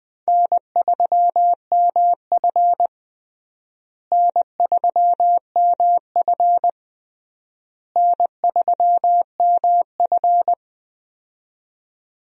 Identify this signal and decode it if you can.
Morse code, heard as a keyed tone, about 20 wpm, 690 Hz. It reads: N3MF N3MF N3MF
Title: USA Call Signs - 4 Characters - Repeated 3x - 20wpm